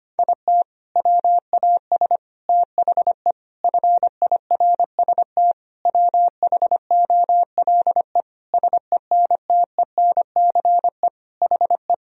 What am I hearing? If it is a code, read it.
Morse code, heard as a keyed tone, about 25 wpm, 700 Hz. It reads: IT WAS THE FIRST WHOLE SENTENCE HE
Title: Sentences from Top 1000 Words - Review Part 1 - 25wpm